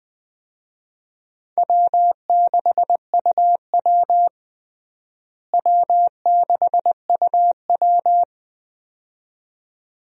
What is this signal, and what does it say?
Morse code, heard as a keyed tone, about 20 wpm, 700 Hz. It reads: W6UW W6UW
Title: USA Call Signs - 4 Characters - Repeated 2x - 20wpm